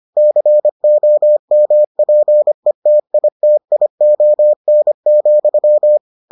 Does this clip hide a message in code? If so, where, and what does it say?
Yes: Morse code from 0.2 to 6.0 s, heard as a keyed tone, about 25 wpm, 600 Hz. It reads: COMPETITION,